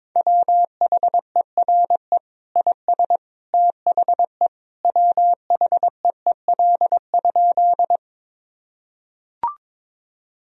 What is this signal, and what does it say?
Morse code, heard as a keyed tone, about 22 wpm, 700 Hz. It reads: WHERE IS THE WHEEL?
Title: Sentences from Top 1000 Words - Review Part 3 - 22wpm